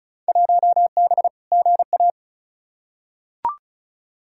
Morse code, heard as a keyed tone, about 36 wpm, 700 Hz. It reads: 1B GA
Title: Field Day Exchanges - 35wpm